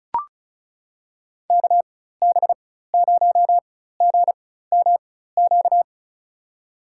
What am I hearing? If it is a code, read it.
Morse code, heard as a keyed tone, about 34 wpm, spaced as for 16 wpm, 700 Hz. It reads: KB0GMQ
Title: USA Call Signs - 6 Characters - 4x Character Spacing - 35wpm